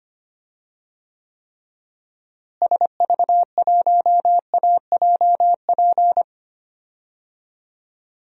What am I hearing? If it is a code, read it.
Morse code, heard as a keyed tone, about 25 wpm, 700 Hz. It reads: SV1AJP
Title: DXCC-100 Call Signs - 6 Characters - Repeated 3x - 25wpm